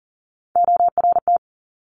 Morse code, heard as a keyed tone, about 38 wpm, 700 Hz. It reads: ORT
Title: Top Three Letter Combinations in Words 40wpm